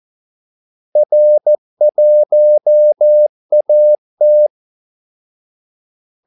Morse code, heard as a keyed tone, about 14 wpm, 600 Hz. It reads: R1AT
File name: Callsigns-14wpm